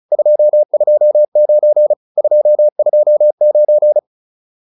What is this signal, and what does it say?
Morse code, heard as a keyed tone, about 34 wpm, 600 Hz. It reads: 229 229
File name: QSO 402 35